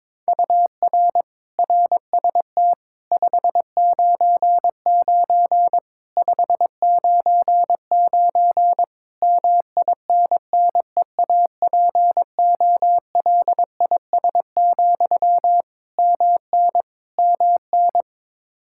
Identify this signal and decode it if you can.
Morse code, heard as a keyed tone, about 22 wpm, 700 Hz. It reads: UR RST 599 599 MINNEAPOLIS, MN MN
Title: Practice QSO - Exchange 1 - Beginning - 22wpm